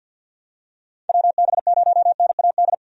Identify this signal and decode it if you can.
Morse code, heard as a keyed tone, about 53 wpm, 700 Hz. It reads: WB0NAD
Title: USA Call Signs - 6 Characters - 50wpm